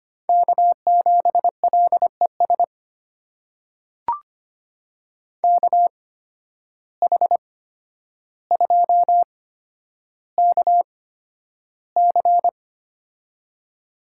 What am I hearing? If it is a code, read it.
Morse code, heard as a keyed tone, about 25 wpm, spaced as for 11 wpm, 700 Hz. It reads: K7LES KH2KC
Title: USA Call Signs - 5 Characters - 8x Character Spacing - 25wpm